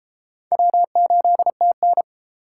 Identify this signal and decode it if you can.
Morse code, heard as a keyed tone, about 38 wpm, 700 Hz. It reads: W8TN